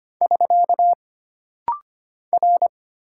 Morse code, heard as a keyed tone, about 25 wpm, 700 Hz. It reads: <SK> R
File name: Single Character (Letter:Number:Punctuation:Prosign) - Rapid-Fire - 25wpm